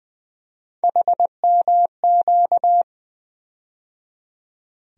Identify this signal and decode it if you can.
Morse code, heard as a keyed tone, about 20 wpm, 700 Hz. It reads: HMQ
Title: USA Call Sign Postfixes - 3 Letters 20wpm